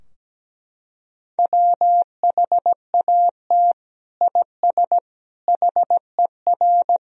Morse code, heard as a keyed tone, about 17 wpm, 700 Hz. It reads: WHAT IS HER
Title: Sentences from Top 1000 Words - Review Part 2 - 17wpm